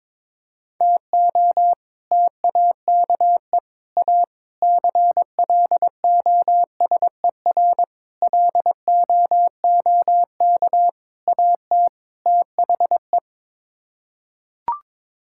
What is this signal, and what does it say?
Morse code, heard as a keyed tone, about 22 wpm, 700 Hz. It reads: TO TAKE A CLOSER LOOK AT THE
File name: practice / 22wpm / Sets of 7 Words - 22wpm